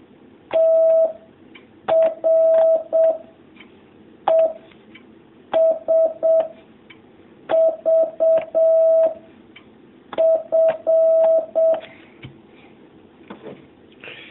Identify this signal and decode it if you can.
Morse code, heard as a keyed tone, about 7 wpm, 650 Hz. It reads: T R E S V F
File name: jh8cbh / 2019cw